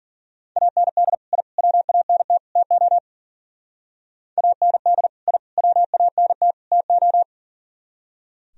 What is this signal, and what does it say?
Morse code, heard as a keyed tone, about 52 wpm, 700 Hz. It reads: AND I WANT TO AND I WANT TO